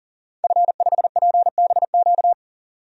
Morse code, heard as a keyed tone, about 40 wpm, 700 Hz. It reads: F5PBQ
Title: DXCC-100 Call Signs - 5 Characters - 40wpm